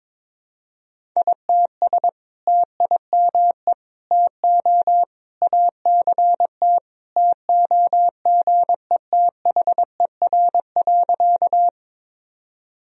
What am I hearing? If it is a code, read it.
Morse code, heard as a keyed tone, about 22 wpm, 700 Hz. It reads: ITS TIME TO ACT TOGETHER.
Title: Sentences from Top 1000 Words - Review Part 1 - 22wpm